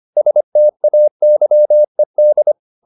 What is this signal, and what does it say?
Morse code, heard as a keyed tone, about 25 wpm, 600 Hz. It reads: STAYED